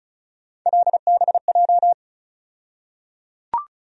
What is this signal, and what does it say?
Morse code, heard as a keyed tone, about 36 wpm, 700 Hz. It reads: LBJ
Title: USA Call Sign Postfixes - 3 Letters 35wpm